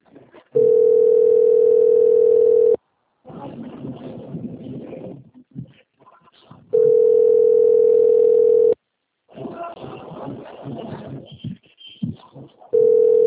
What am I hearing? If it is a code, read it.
Ringback tone (North America), 440 + 480 Hz.
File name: out- - - - - .25994